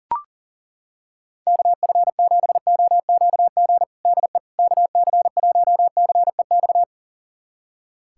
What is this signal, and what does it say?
Morse code, heard as a keyed tone, about 37 wpm, 700 Hz. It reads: KF7OQG DE KC1CEX
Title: USA Callsign DE Callsign - 6 Characters - 40wpm